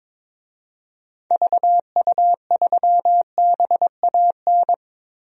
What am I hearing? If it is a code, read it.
Morse code, heard as a keyed tone, about 22 wpm, 700 Hz. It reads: VU3BAN
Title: DXCC-100 Call Signs - 6 Characters - 22wpm